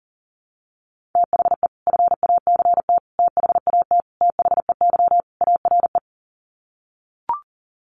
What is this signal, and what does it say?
Morse code, heard as a keyed tone, about 40 wpm, 700 Hz. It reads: THE FACT THAT THEY ARE
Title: Sets of 5 Words Encore2 40wpm